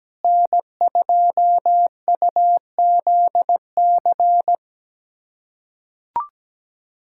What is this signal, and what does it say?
Morse code, heard as a keyed tone, about 17 wpm, 700 Hz. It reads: N2UZC
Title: USA Callsign DE Callsign - 5 Characters - 17wpm